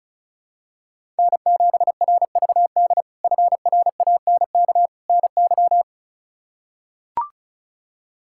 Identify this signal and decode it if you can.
Morse code, heard as a keyed tone, about 36 wpm, 700 Hz. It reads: N7RVD FRANK NY
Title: K1USN SST QSO Practice - 35wpm